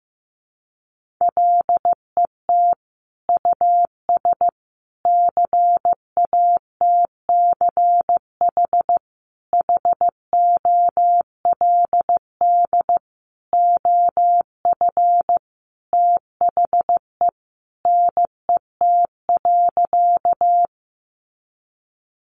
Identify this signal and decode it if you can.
Morse code, heard as a keyed tone, about 15 wpm, 700 Hz. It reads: LET US CATCH HOLD OF THE NET.
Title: Sentences from Top 1500 Words 15wpm